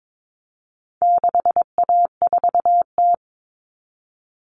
Morse code, heard as a keyed tone, about 22 wpm, 700 Hz. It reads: 6A4T
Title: Common Cut Numbers - 4-Digit 22wpm